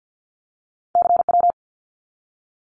Morse code, heard as a keyed tone, about 50 wpm, 700 Hz. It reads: CW
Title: Abbreviations 50wpm